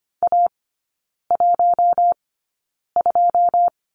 Morse code, heard as a keyed tone, about 25 wpm, spaced as for 9 wpm, 690 Hz. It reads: A12
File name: Lesson 14e - Number 2 - Three Characters 25wpm